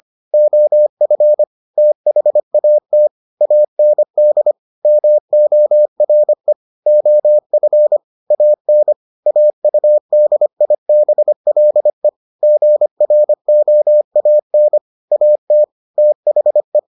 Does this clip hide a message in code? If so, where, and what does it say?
Yes: Morse code from 0.3 to 16.8 s, heard as a keyed tone, about 25 wpm, 600 Hz. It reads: OF THAT AND MORE OF AN AUDIBLE GROAN AT THE